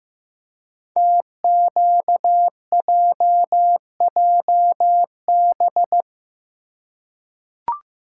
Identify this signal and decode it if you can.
Morse code, heard as a keyed tone, about 15 wpm, 700 Hz. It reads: TQJJB
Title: Code Groups - 5 Letter:Numbers - 15wpm